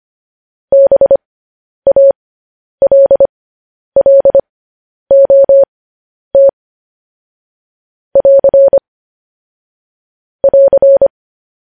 Morse code, heard as a keyed tone, about 25 wpm, spaced as for 10 wpm, 550 Hz. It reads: BALLOT <AR> <AR>
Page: news headlines in morse code at 10 wpm